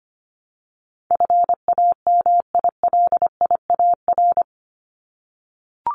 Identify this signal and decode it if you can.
Morse code, heard as a keyed tone, about 25 wpm, 700 Hz. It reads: FAMILIAR
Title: Top 751-1000 Words 25wpm